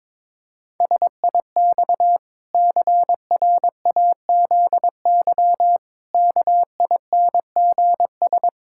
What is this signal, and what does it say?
Morse code, heard as a keyed tone, about 22 wpm, 700 Hz. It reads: SIX CRAZY KINGS
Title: Pangram Challenge v2 - 22wpm